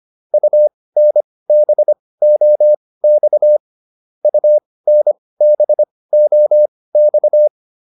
Morse code, heard as a keyed tone, about 25 wpm, spaced as for 18 wpm, 600 Hz. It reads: UNBOX UNBOX